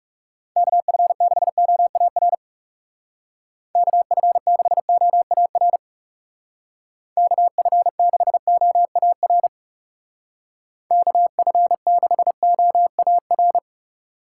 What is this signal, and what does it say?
Morse code, heard as a keyed tone, about 40 wpm, 700 Hz. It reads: KF6OAR KF6OAR KF6OAR KF6OAR